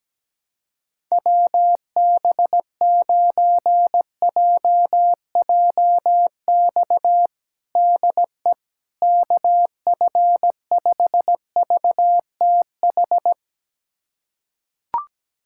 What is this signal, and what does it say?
Morse code, heard as a keyed tone, about 17 wpm, 700 Hz. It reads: WB9JJX DE KF5VTH
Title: USA Callsign DE Callsign - 6 Characters - 17wpm